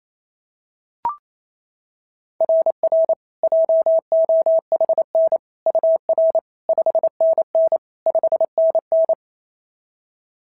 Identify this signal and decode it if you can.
Morse code, heard as a keyed tone, about 27 wpm, 690 Hz. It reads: RR JOHN UR 5NN 5NN
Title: SOTA QSO Practice 28wpm